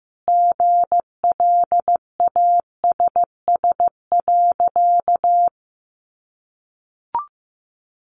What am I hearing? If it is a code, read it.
Morse code, heard as a keyed tone, about 15 wpm, 700 Hz. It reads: GLASS.
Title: Sentences from Top 1300 Words 15wpm